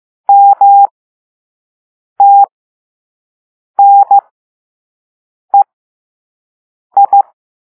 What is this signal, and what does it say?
Morse code, heard as a keyed tone, about 15 wpm, spaced as for 5 wpm, 800 Hz. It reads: MTNEI